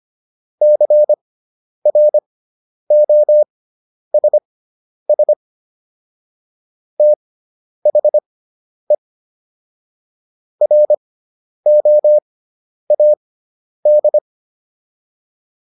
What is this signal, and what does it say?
Morse code, heard as a keyed tone, about 25 wpm, spaced as for 10 wpm, 600 Hz. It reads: CROSS THE ROAD